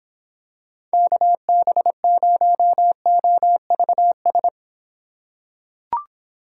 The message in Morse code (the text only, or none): KB0OVS